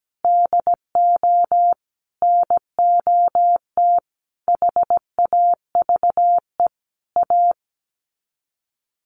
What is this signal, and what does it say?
Morse code, heard as a keyed tone, about 17 wpm, 700 Hz. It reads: DO NOT HAVE A
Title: Sets of 5 Words 17wpm